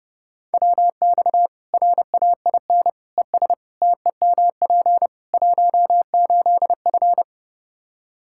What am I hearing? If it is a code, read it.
Morse code, heard as a keyed tone, about 32 wpm, 700 Hz. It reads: WX RAIN ES TEMP 18F
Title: K1USN SST QSO Practice 30wpm